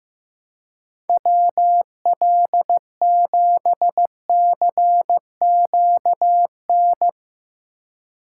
Morse code, heard as a keyed tone, about 15 wpm, 700 Hz. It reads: WL7CQN
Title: USA Call Signs - 6 Characters - 4x Character Spacing - 15wpm